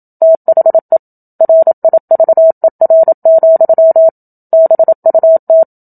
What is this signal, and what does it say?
Morse code, heard as a keyed tone, about 27 wpm, 650 Hz. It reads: THE RIVER, BUT
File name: Book Two / 01-Chapter 1 - 27 WPM